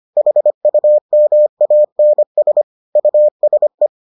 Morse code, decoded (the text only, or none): HUMANS USE